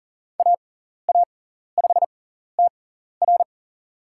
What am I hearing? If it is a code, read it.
Morse code, heard as a keyed tone, about 40 wpm, spaced as for 14 wpm, 700 Hz. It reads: AA5TR
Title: USA Call Signs - 5 Characters - 6x Character Spacing - 40wpm